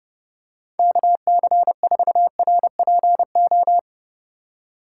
Morse code, heard as a keyed tone, about 28 wpm, 700 Hz. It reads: KC4RPO